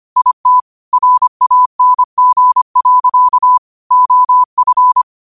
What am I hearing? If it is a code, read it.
Morse code, heard as a keyed tone, about 25 wpm, 1000 Hz. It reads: IT RANG. OF
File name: qotdfiles / 25wpm